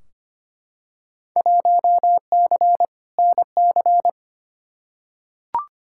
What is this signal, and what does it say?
Morse code, heard as a keyed tone, about 25 wpm, 700 Hz. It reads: 1C NC
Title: Field Day Exchanges - 25wpm